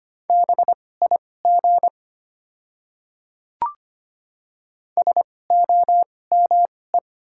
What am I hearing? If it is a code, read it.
Morse code, heard as a keyed tone, about 25 wpm, spaced as for 18 wpm, 700 Hz. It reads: BIG SOME
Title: 2x Character Spacing - Top 200 Words - 25wpm